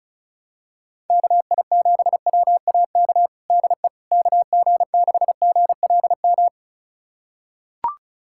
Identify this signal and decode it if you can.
Morse code, heard as a keyed tone, about 34 wpm, 700 Hz. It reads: KI7WAK DE KG6GLM
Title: USA Callsign DE Callsign - 6 Characters - 35wpm